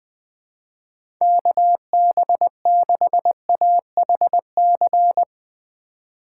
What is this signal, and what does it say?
Morse code, heard as a keyed tone, about 20 wpm, 700 Hz. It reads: KB6AHC
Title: USA Call Signs - 6 Characters - 3x Character Spacing - 20wpm